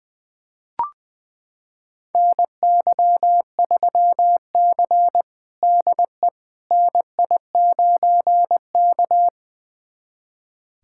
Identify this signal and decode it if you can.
Morse code, heard as a keyed tone, about 20 wpm, 700 Hz. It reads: NY3C DE NI9K
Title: USA Callsign DE Callsign - 4 Characters - 20wpm